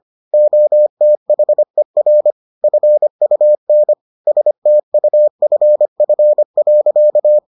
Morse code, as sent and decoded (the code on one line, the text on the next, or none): --- - .... .-. / ..-. ..- -. / ... - ..- ..-. ..-. .-.-.-
OTHER FUN STUFF.